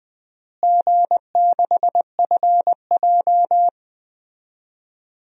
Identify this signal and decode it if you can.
Morse code, heard as a keyed tone, about 21 wpm, 700 Hz. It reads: G6FJ
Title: DXCC-100 Call Signs - 4 Characters - Repeated 3x - 20wpm